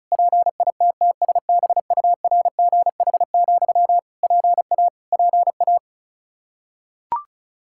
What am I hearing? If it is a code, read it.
Morse code, heard as a keyed tone, about 36 wpm, 700 Hz. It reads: PITTSBURGH, PA PA
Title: Practice QSO - Exchange 1 - Beginning - 35wpm